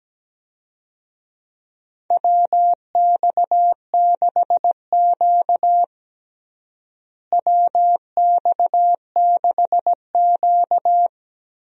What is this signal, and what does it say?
Morse code, heard as a keyed tone, about 17 wpm, 700 Hz. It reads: WX6Q WX6Q
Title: USA Call Signs - 4 Characters - Repeated 2x - 17wpm